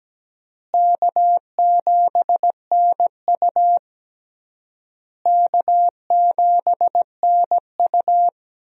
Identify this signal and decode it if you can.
Morse code, heard as a keyed tone, about 17 wpm, 700 Hz. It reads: K7NU K7NU
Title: USA Call Signs - 4 Characters - Repeated 3x - 17wpm